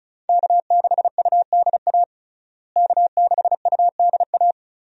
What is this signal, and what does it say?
Morse code, heard as a keyed tone, about 34 wpm, 700 Hz. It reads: K6UDA K6UDA
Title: USA Call Signs - 5 Characters - Repeated 2x - 35wpm